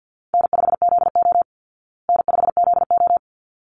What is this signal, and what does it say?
Morse code, heard as a keyed tone, about 50 wpm, 700 Hz. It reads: N5ZO N5ZO
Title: Call Signs Repeated 3x - 50wpm